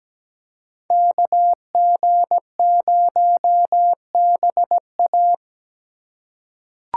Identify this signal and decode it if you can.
Morse code, heard as a keyed tone, about 17 wpm, 700 Hz. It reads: KG0BA
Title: USA Call Signs - 5 Characters - 4x Character Spacing - 17wpm